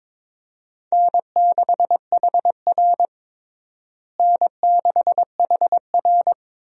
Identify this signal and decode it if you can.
Morse code, heard as a keyed tone, about 21 wpm, 700 Hz. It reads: N6HR N6HR
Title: USA Call Signs - 4 Characters - Repeated 2x - 22wpm